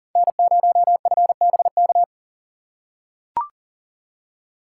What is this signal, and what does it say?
Morse code, heard as a keyed tone, about 37 wpm, 700 Hz. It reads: N0FBK
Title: USA Call Signs - 5 Characters - 4x Character Spacing - 40wpm